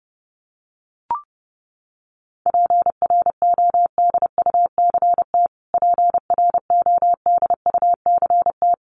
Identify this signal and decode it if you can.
Morse code, heard as a keyed tone, about 29 wpm, 700 Hz. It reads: PRODUCT PRODUCT
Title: Top 500 Words Repeated 30wpm